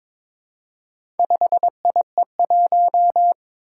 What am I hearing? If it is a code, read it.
Morse code, heard as a keyed tone, about 22 wpm, 700 Hz. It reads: 5IE1
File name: practice / 22wpm / Four Letter-Number Combinations - 22wpm